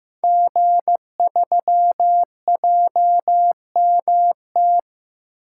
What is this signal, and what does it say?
Morse code, heard as a keyed tone, about 15 wpm, 700 Hz. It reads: G3JMT
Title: Code Groups - 5 Letter:Numbers - 15wpm